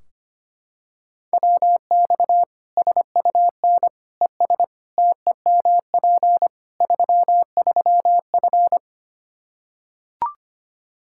Morse code, heard as a keyed tone, about 25 wpm, 700 Hz. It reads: WX SUN ES TEMP 33F
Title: K1USN SST QSO Practice 25wpm